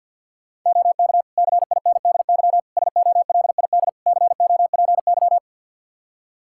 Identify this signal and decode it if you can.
Morse code, heard as a keyed tone, about 51 wpm, 700 Hz. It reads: OK CINDY SOLID COPY
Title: K1USN SST QSO Practice 50wpm